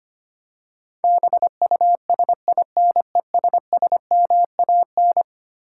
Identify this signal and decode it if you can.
Morse code, heard as a keyed tone, about 25 wpm, 700 Hz. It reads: BUSINESSMAN
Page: morse-code-ninja-practice-25wpm